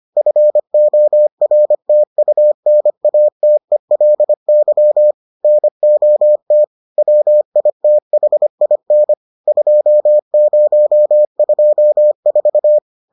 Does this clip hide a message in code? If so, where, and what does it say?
Yes: Morse code from 0.2 to 12.8 s, heard as a keyed tone, about 25 wpm, 600 Hz. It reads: FORTUNATELY NOT WITHIN 2024